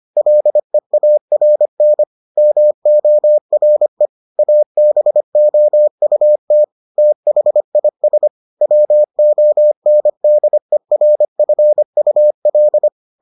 Morse code, heard as a keyed tone, about 25 wpm, 600 Hz. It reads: LEARN MORE ABOUT THIS WONDERFUL